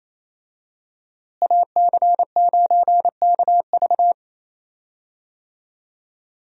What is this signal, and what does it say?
Morse code, heard as a keyed tone, about 27 wpm, 700 Hz. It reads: AC9KV
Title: USA Call Signs - 5 Characters - Repeated 2x - 28wpm